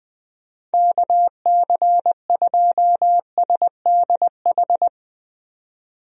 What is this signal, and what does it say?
Morse code, heard as a keyed tone, about 20 wpm, 700 Hz. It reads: KC2SDH
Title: USA Call Signs - 6 Characters - 20wpm